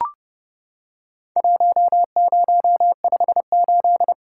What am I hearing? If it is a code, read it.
Morse code, heard as a keyed tone, about 32 wpm, 700 Hz. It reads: 1058
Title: Worked All Europe DX Contest - QTC Exchange - 30wpm